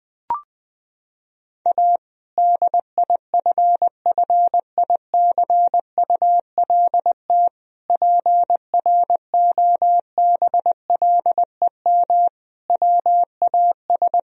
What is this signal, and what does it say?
Morse code, heard as a keyed tone, about 20 wpm, 700 Hz. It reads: A DIFFICULT PROBLEM WAS
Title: Sentences from Top 1000 Words - Review Part 2 - 20wpm